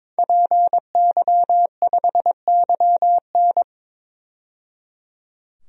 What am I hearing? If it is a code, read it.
Morse code, heard as a keyed tone, about 22 wpm, 700 Hz. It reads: PY5YN
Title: DXCC-100 Call Signs - 5 Characters - 22wpm